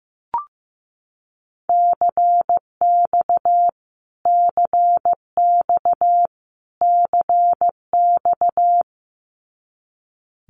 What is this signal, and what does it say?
Morse code, heard as a keyed tone, about 15 wpm, 700 Hz. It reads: CX CX CX
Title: Abbreviations Repeated 3x - 15wpm